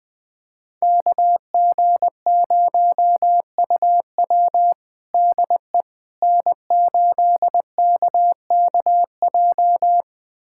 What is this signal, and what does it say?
Morse code, heard as a keyed tone, about 20 wpm, 700 Hz. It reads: KG0UW DE N8KKJ
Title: USA Callsign DE Callsign - 5 Characters - 20wpm